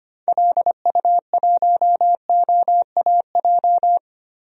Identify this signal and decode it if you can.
Morse code, heard as a keyed tone, about 25 wpm, 700 Hz. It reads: LU1OAJ
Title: DXCC-100 Call Signs - 6 Characters - Repeated 2x - 25wpm